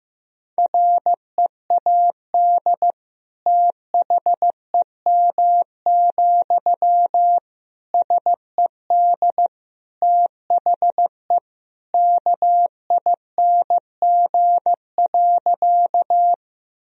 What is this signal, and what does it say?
Morse code, heard as a keyed tone, about 15 wpm, 700 Hz. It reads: READ THEM, SED THE KING.
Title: Sentences from Top 400 Words and Abbreviations 15wpm